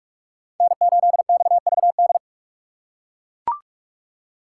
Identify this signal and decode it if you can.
Morse code, heard as a keyed tone, about 44 wpm, 700 Hz. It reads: N8XVD